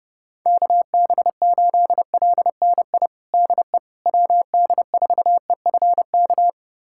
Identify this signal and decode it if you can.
Morse code, heard as a keyed tone, about 32 wpm, 700 Hz. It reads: KB8LNI DE WD4EFK
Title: USA Callsign DE Callsign 30wpm